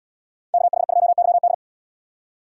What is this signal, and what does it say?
Morse code, heard as a keyed tone, about 77 wpm, 700 Hz. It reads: KH2YD